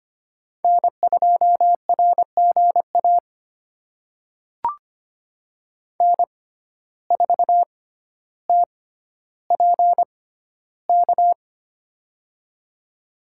Morse code, heard as a keyed tone, about 25 wpm, spaced as for 13 wpm, 700 Hz. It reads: N2RGA N4TPK
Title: USA Call Signs - 5 Characters - 6x Character Spacing - 25wpm